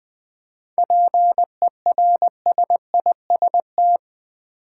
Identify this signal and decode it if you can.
Morse code, heard as a keyed tone, about 20 wpm, 700 Hz. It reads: PERSIST
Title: 7 Letter Words - First 3 Letters Given - 20wpm